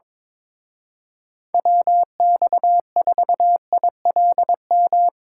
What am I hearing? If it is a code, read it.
Morse code, heard as a keyed tone, about 22 wpm, 700 Hz. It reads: WX4ILM
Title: Speed Racing - USA Call Signs 22wpm